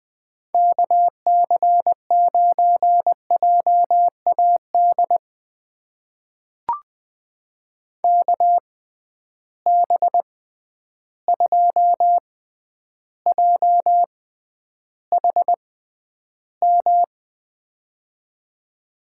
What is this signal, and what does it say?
Morse code, heard as a keyed tone, about 21 wpm, spaced as for 10 wpm, 700 Hz. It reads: KC9JAD KB2JHM
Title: USA Call Signs - 6 Characters - 6x Character Spacing - 20wpm